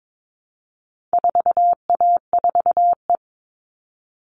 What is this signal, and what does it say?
Morse code, heard as a keyed tone, about 22 wpm, 700 Hz. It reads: 4A4E
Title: All Cut Numbers - 4-Digit 22wpm